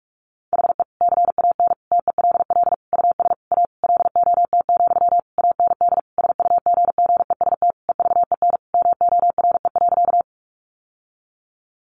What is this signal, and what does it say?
Morse code, heard as a keyed tone, about 44 wpm, 700 Hz. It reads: HE CAN TELL US A LOT, AND SUGGEST EVEN MORE.
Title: Sentences from Top 1300 Words 45wpm